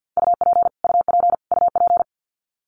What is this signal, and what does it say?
Morse code, heard as a keyed tone, about 50 wpm, 700 Hz. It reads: UP UP UP
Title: Top 200 Words Repeated 50wpm